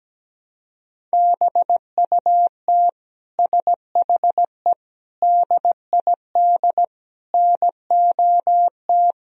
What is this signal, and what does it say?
Morse code, heard as a keyed tone, about 17 wpm, 700 Hz. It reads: BUT SHE DID NOT
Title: Sentences from Top 1000 Words - Review Part 1 - 17wpm